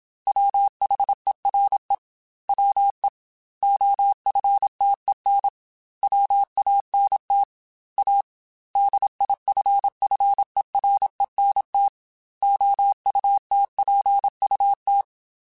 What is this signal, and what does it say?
Morse code, heard as a keyed tone, about 27 wpm, 800 Hz. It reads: WHERE WE OFTEN WANT A DIFFERENT OUTPUT
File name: Technical Subjects / Technical Subjects-mp3s / Sw Pwr Sup @25